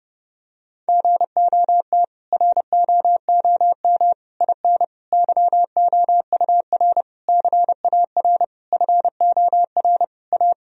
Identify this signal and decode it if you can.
Morse code, heard as a keyed tone, about 30 wpm, 700 Hz. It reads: GOT ROOM IN YOUR CAR FOR A
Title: Sentences from Top 1000 Words - Review Part 1 - 30wpm